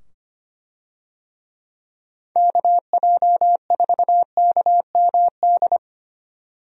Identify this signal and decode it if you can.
Morse code, heard as a keyed tone, about 25 wpm, 700 Hz. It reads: KJ4KMD
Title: USA Call Signs - 6 Characters - 25wpm